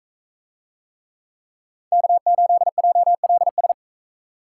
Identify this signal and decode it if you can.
Morse code, heard as a keyed tone, about 44 wpm, 700 Hz. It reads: K8JLS